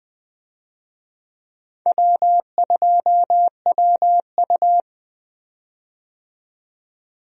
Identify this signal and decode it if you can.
Morse code, heard as a keyed tone, about 20 wpm, 700 Hz. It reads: W2WU
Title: USA Call Signs - 4 Characters - 20wpm